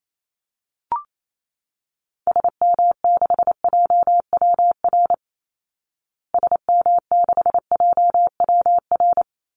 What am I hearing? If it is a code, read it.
Morse code, heard as a keyed tone, about 28 wpm, 700 Hz. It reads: SM6JWR SM6JWR
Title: Call Signs Repeated 3x - 28wpm